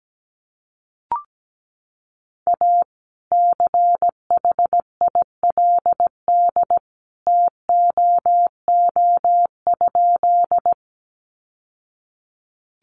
Morse code, heard as a keyed tone, about 17 wpm, 700 Hz. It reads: A CHILD TOO?
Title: Sentences from Top 800 Words 17wpm